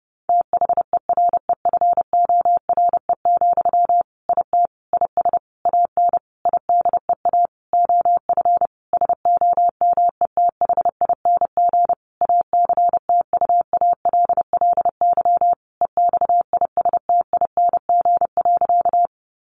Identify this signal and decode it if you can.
Morse code, heard as a keyed tone, about 30 wpm, 700 Hz. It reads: THEREFORE, IT IS AN IDEA OF SOMETHING ACTUALLY EXISTING.
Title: Sentences from Top 1500 Words 30wpm